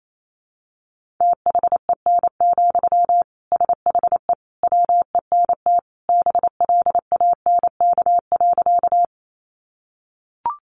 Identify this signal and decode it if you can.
Morse code, heard as a keyed tone, about 28 wpm, 700 Hz. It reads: THEN, SHE WENT BLANK.
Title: Sentences from Top 1500 Words 28wpm